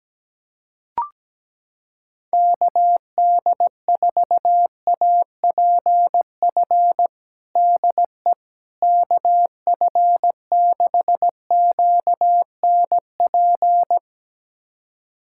Morse code, heard as a keyed tone, about 17 wpm, 700 Hz. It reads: KD4APF DE KF6QNP
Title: USA Callsign DE Callsign - 6 Characters - 17wpm